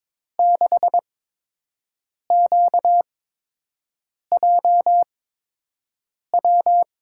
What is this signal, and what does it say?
Morse code, heard as a keyed tone, about 22 wpm, spaced as for 6 wpm, 700 Hz. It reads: 6QJW
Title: USA Call Signs - 6 Characters - 8x Character Spacing - 22wpm